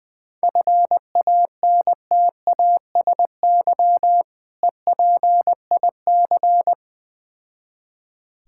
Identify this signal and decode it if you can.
Morse code, heard as a keyed tone, about 20 wpm, 700 Hz. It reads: FANTASY EPIC
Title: Difficult Words with Hints - 20wpm